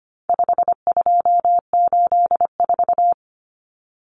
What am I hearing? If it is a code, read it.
Morse code, heard as a keyed tone, about 25 wpm, 700 Hz. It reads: 5284
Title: Common Cut Numbers - 4-Digit 25wpm